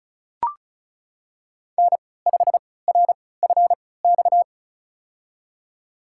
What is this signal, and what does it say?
Morse code, heard as a keyed tone, about 34 wpm, spaced as for 20 wpm, 700 Hz. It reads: N5RFX
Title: USA Call Signs - 5 Characters - 3x Character Spacing - 35wpm